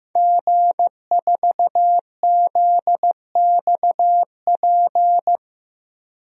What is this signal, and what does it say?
Morse code, heard as a keyed tone, about 15 wpm, 700 Hz. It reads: G4ZXP
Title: DXCC-100 Call Signs - 5 Characters - 15wpm